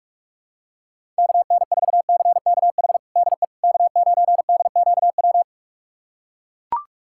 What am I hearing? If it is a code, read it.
Morse code, heard as a keyed tone, about 41 wpm, 700 Hz. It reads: KN4CKH DE K9DQW
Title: USA Callsign DE Callsign 45wpm